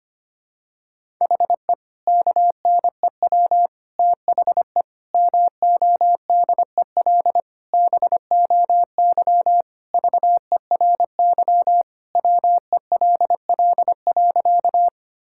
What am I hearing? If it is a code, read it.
Morse code, heard as a keyed tone, about 25 wpm, 700 Hz. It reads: HE KNEW THE MODEL BOY VERY WELL.
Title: Sentences from Top 1000 Words - Review Part 5 - 25wpm